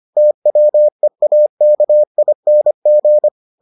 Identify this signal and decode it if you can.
Morse code, heard as a keyed tone, about 25 wpm, 600 Hz. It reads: TWEAKING